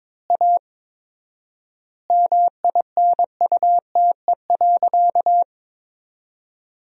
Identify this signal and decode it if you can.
Morse code, heard as a keyed tone, about 21 wpm, 700 Hz. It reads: A MINUTE.
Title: 4x Word-Spacing - Sentences from Top 500 Words - 22wpm